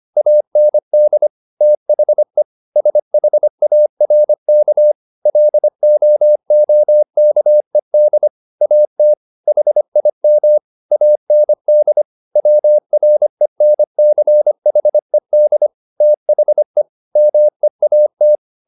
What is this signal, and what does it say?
Morse code, heard as a keyed tone, about 25 wpm, 600 Hz. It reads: AND THE SHARK LOOKED AT HIM AND WRENCHED THE MEAT